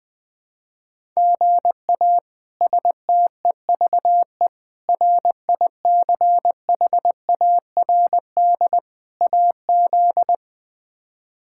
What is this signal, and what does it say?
Morse code, heard as a keyed tone, about 20 wpm, 700 Hz. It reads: GA STEVE RICHARD AZ
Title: K1USN SST QSO Practice - 20wpm